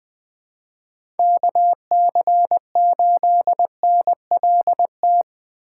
Morse code, heard as a keyed tone, about 20 wpm, 700 Hz. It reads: KC8NLT
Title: USA Call Signs - 8x Character Spacing - 20wpm